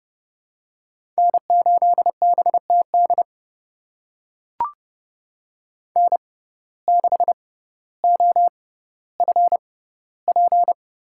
Morse code, heard as a keyed tone, about 30 wpm, spaced as for 15 wpm, 700 Hz. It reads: N8BTD N6OFP
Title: USA Call Signs - 5 Characters - 6x Character Spacing - 30wpm